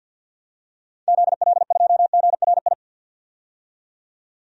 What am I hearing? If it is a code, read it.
Morse code, heard as a keyed tone, about 52 wpm, 700 Hz. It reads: ZL2GRI
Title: DXCC-100 Call Signs - 6 Characters - 50wpm